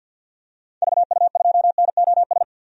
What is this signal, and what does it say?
Morse code, heard as a keyed tone, about 51 wpm, 700 Hz. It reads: VU2NOS
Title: DXCC-100 Call Signs - 6 Characters - Repeated 2x - 50wpm